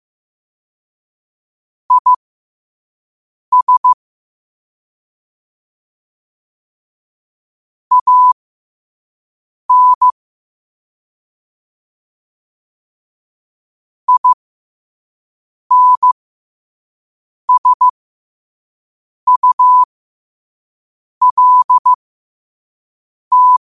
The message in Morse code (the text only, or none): IS AN INSULT